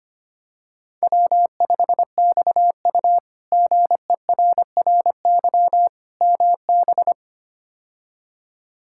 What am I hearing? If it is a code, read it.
Morse code, heard as a keyed tone, about 25 wpm, 700 Hz. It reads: W5XU GERRY MB
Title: K1USN SST QSO Practice - 25wpm